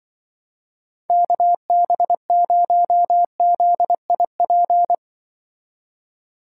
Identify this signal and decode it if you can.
Morse code, heard as a keyed tone, about 23 wpm, 700 Hz. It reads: KB0ZIP
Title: Speed Racing - USA Call Signs 20wpm